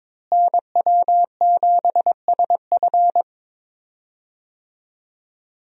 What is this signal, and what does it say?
Morse code, heard as a keyed tone, about 22 wpm, 700 Hz. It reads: NW7SF
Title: USA Call Signs - 5 Characters - Repeated 3x - 22wpm